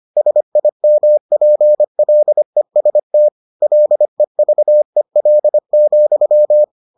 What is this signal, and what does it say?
Morse code, heard as a keyed tone, about 25 wpm, 600 Hz. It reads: SIMPLEST LEVEL,